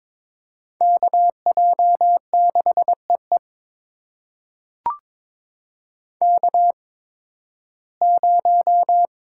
Morse code, heard as a keyed tone, about 22 wpm, spaced as for 14 wpm, 700 Hz. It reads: KJ6EE K0
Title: USA Call Signs - 5 Characters - 8x Character Spacing - 22wpm